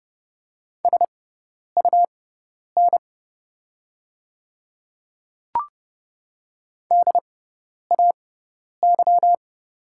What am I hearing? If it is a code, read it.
Morse code, heard as a keyed tone, about 30 wpm, spaced as for 10 wpm, 700 Hz. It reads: SUN DAY
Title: 6x Character Spacing - Top 300 Words - 30wpm